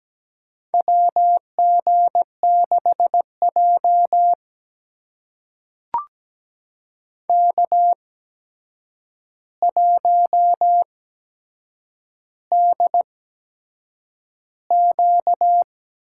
Morse code, heard as a keyed tone, about 17 wpm, spaced as for 7 wpm, 700 Hz. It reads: WG6J K1DQ
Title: USA Call Signs - 4 Characters - 8x Character Spacing - 17wpm